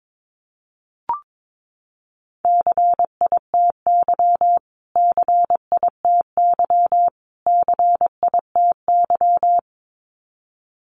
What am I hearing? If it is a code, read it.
Morse code, heard as a keyed tone, about 22 wpm, 700 Hz. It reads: CITY CITY CITY
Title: Top 300 Words Repeated 22wpm